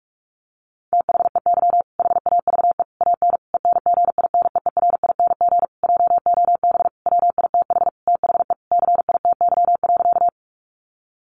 Morse code, heard as a keyed tone, about 44 wpm, 700 Hz. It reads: THEY HAVE AN ENGINEERING JOB WITH THE CITY.
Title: Sentences from Top 1500 Words 45wpm